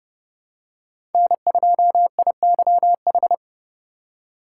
Morse code, heard as a keyed tone, about 29 wpm, 700 Hz. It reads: N2IYH
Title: USA Call Signs - 5 Characters - 6x Character Spacing - 30wpm